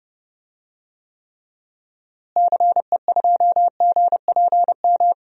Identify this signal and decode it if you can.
Morse code, heard as a keyed tone, about 30 wpm, 700 Hz. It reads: CE2GPM
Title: DXCC-100 Call Signs - 6 Characters - Repeated 3x - 30wpm